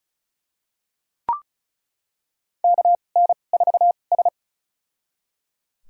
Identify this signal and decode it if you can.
Morse code, heard as a keyed tone, about 36 wpm, spaced as for 25 wpm, 700 Hz. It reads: KN4S
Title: USA Call Signs - 4 Characters - 2x Character Spacing - 35wpm